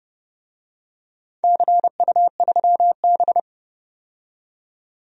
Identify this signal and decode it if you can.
Morse code, heard as a keyed tone, about 30 wpm, 700 Hz. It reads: CU3B
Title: DXCC-100 Call Signs - 4 Characters - 30wpm